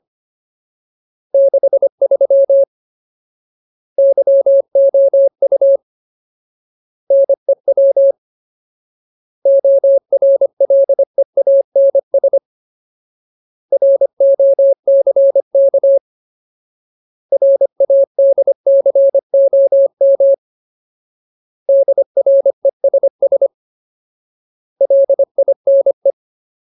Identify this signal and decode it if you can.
Morse code, heard as a keyed tone, about 25 wpm, 550 Hz. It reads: 63 YOU NEW ORLEANS ROCK RADCOM DRESS LINE
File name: Ultimate Word List / 25WPM Ultimate